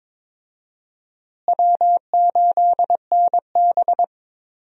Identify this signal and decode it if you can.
Morse code, heard as a keyed tone, about 22 wpm, 700 Hz. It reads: W8NB
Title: USA Call Signs - 4 Characters - Repeated 2x - 22wpm